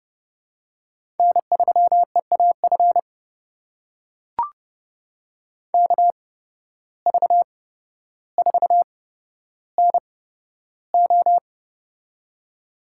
Morse code, heard as a keyed tone, about 30 wpm, spaced as for 13 wpm, 700 Hz. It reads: N3EAF KV4NO
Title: USA Call Signs - 5 Characters - 8x Character Spacing - 30wpm